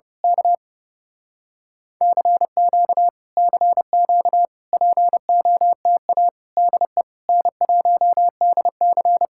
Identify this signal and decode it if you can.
Morse code, heard as a keyed tone, about 29 wpm, 700 Hz. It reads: K CQ CQ POTA DE N1DC
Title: Speed-Racing Common QSO Phrases - 30wpm